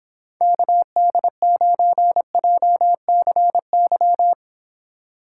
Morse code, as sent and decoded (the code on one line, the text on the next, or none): -.- -.. ----. .--- -.-. -.--
KD9JCY